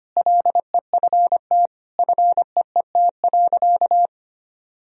Morse code, heard as a keyed tone, about 25 wpm, 700 Hz. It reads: LEFT FEET.
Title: Sentences from Top 1000 Words - Review Part 1 - 25wpm